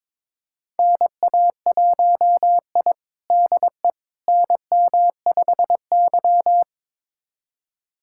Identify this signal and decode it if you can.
Morse code, heard as a keyed tone, about 21 wpm, 700 Hz. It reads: NA1I DE NM5Y
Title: USA Callsign DE Callsign - 4 Characters - 22wpm